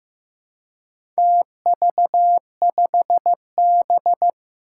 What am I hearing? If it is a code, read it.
Morse code, heard as a keyed tone, about 15 wpm, 700 Hz. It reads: TV5B
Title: Four Challenging Letter-Number Combinations 15wpm